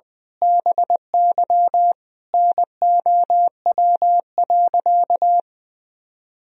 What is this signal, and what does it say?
Morse code, heard as a keyed tone, about 20 wpm, 700 Hz. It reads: BY NOW.
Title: Sentences from Top 1000 Words - Review Part 3 - 20wpm